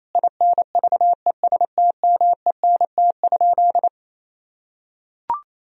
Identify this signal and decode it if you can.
Morse code, heard as a keyed tone, about 28 wpm, 700 Hz. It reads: INVESTMENT?
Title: Sentences from Top 1000 Words - Review Part 4 - 28wpm